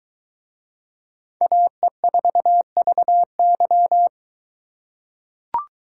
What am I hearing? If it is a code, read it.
Morse code, heard as a keyed tone, about 23 wpm, 700 Hz. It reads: AE4VY